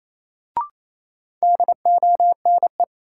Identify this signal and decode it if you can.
Morse code, heard as a keyed tone, about 28 wpm, 700 Hz. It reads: DONE
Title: Sets of 5 Words Encore - New Word Drill - 28wpm